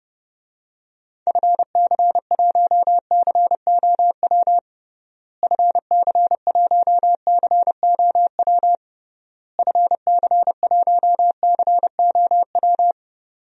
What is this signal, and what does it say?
Morse code, heard as a keyed tone, about 32 wpm, 700 Hz. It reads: FC1COW FC1COW FC1COW
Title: DXCC-100 Call Signs - 6 Characters - Repeated 3x - 30wpm